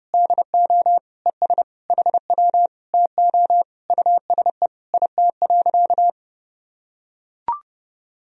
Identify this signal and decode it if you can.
Morse code, heard as a keyed tone, about 30 wpm, 700 Hz. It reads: DO ES HW TO USE IT.
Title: Sentences from Top 500 Words and Abbreviations 30wpm